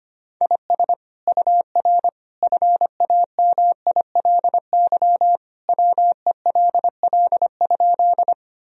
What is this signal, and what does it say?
Morse code, heard as a keyed tone, about 25 wpm, 700 Hz. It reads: IS UR FAMILY WELL?
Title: Sentences from Top 400 Words and Abbreviations 25wpm